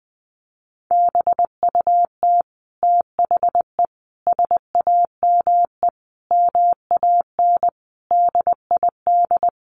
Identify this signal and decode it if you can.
Morse code, heard as a keyed tone, about 20 wpm, 700 Hz. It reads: BUT THE SAME MAN DID
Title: A Sound Of Thunder 20wpm